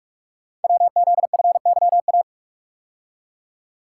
Morse code, heard as a keyed tone, about 44 wpm, 700 Hz. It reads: WZFYA
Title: Code Groups - 5 Letters - 45wpm